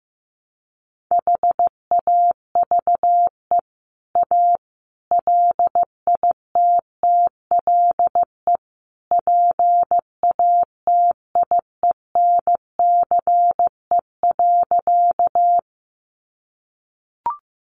Morse code, heard as a keyed tone, about 15 wpm, 700 Hz. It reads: HAVE A LITTLE PATIENCE.
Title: Sentences from Top 1300 Words 15wpm